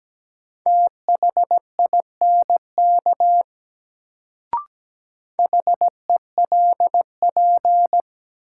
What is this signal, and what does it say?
Morse code, heard as a keyed tone, about 17 wpm, 700 Hz. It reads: THINK HELP
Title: Sets of 2 Words - New Word Drill - 17wpm